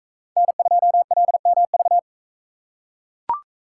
Morse code, heard as a keyed tone, about 43 wpm, 700 Hz. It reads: N2LMV